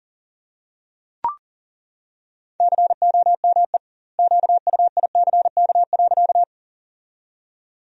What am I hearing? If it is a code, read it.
Morse code, heard as a keyed tone, about 40 wpm, 700 Hz. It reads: COME QUICK.
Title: 2x Word-Spacing - Sentences from Top 500 Words - 40wpm